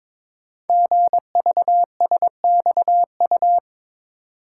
Morse code, heard as a keyed tone, about 22 wpm, 700 Hz. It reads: GVSXU
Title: Code Groups - 5 Letters - 22wpm